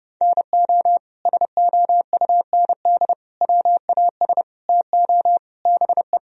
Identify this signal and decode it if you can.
Morse code, heard as a keyed tone, about 30 wpm, 700 Hz. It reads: NO SOUND WAS TO BE
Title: Sentences from Top 1000 Words - Review Part 1 - 30wpm